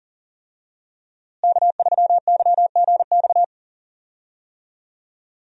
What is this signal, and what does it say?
Morse code, heard as a keyed tone, about 37 wpm, 700 Hz. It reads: K3YGX
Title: USA Call Signs - 5 Characters - Repeated 2x - 40wpm